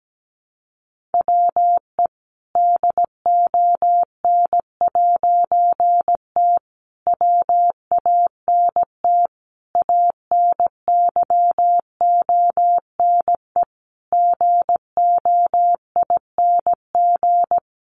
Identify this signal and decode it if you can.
Morse code, heard as a keyed tone, about 17 wpm, 700 Hz. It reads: WE DON'T WANT ANYONE GOING
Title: A Sound Of Thunder 17wpm